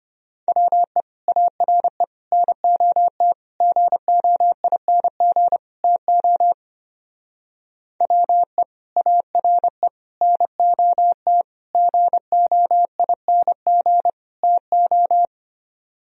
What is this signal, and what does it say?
Morse code, heard as a keyed tone, about 26 wpm, 700 Hz. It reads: WE ARE NOT GOING TO WE ARE NOT GOING TO
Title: Speed-Racing - Sets of 5 Words 25wpm